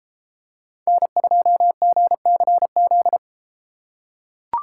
Morse code, heard as a keyed tone, about 32 wpm, 700 Hz. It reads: N2GCZ